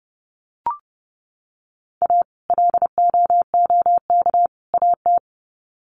Morse code, heard as a keyed tone, about 30 wpm, 700 Hz. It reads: A LOOK AT
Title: Sets of 3 Words 30wpm